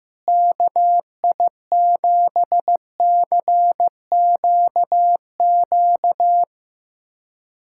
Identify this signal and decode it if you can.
Morse code, heard as a keyed tone, about 16 wpm, 700 Hz. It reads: KI7CQQ
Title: USA Callsign DE Callsign 15wpm